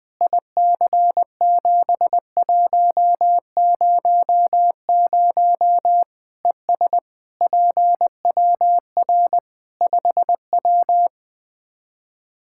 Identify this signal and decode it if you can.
Morse code, heard as a keyed tone, about 21 wpm, 700 Hz. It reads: IC7100 ES PWR 5W
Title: Practice QSO - Exchange 2 - Info - 20wpm